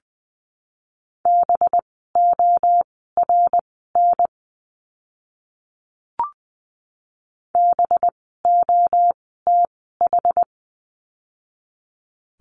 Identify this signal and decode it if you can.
Morse code, heard as a keyed tone, about 20 wpm, spaced as for 14 wpm, 700 Hz. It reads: BORN BOTH
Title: 185 Lesson 30f - Letter b - Words 20wpm